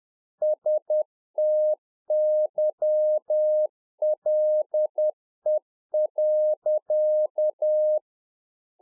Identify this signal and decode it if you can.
Morse code, heard as a keyed tone, about 10 wpm, 610 Hz. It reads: STYLE.